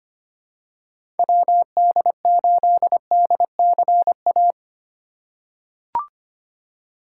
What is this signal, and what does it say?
Morse code, heard as a keyed tone, about 25 wpm, 700 Hz. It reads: WD8DCA
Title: USA Call Signs - 6 Characters - 2x Character Spacing - 25wpm